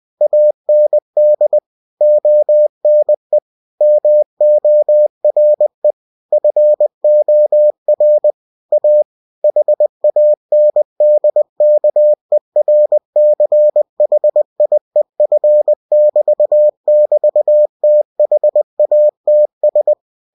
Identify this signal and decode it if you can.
Morse code, heard as a keyed tone, about 20 wpm, 600 Hz. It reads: AND ONE MORE FOR A HANDKERCHIEF <BT> <BT> THATS
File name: cw-books / winnie-the-pooh / winnie-20wpm-page-0060